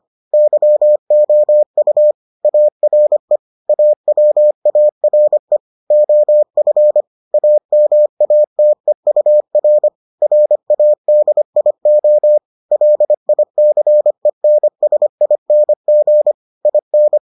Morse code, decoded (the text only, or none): YOU ARE AWARE OF AMATEUR RADIO LICENSING IN